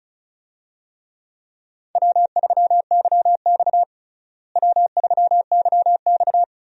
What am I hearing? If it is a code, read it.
Morse code, heard as a keyed tone, about 36 wpm, 700 Hz. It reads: W3YX W3YX
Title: USA Call Signs - 4 Characters - Repeated 3x - 35wpm